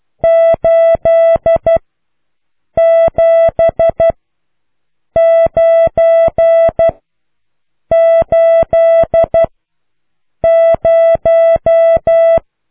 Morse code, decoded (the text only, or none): 87980